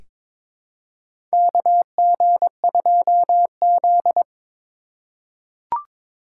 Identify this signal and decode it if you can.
Morse code, heard as a keyed tone, about 21 wpm, 700 Hz. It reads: KG2Z
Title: USA Call Signs - 4 Characters - 4x Character Spacing - 22wpm